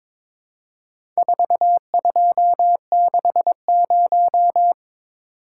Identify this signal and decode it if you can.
Morse code, heard as a keyed tone, about 22 wpm, 700 Hz. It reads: 4260
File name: practice / 22wpm / Four Letter-Number Combinations - 22wpm